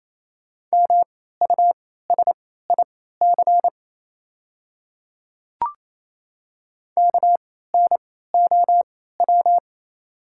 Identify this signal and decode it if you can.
Morse code, heard as a keyed tone, about 28 wpm, spaced as for 16 wpm, 700 Hz. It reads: MUSIC KNOW
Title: 3x Character Spacing - Top 300 Words - 28wpm